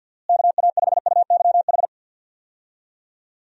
Morse code, heard as a keyed tone, about 51 wpm, 700 Hz. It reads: KA5UYH
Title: USA Call Signs - 6 Characters - Repeated 3x - 50wpm